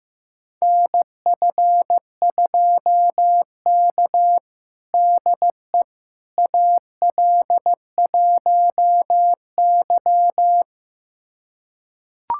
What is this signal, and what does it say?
Morse code, heard as a keyed tone, about 15 wpm, 700 Hz. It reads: NF2K DE AL1Y
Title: USA Callsign DE Callsign - 4 Characters - 15wpm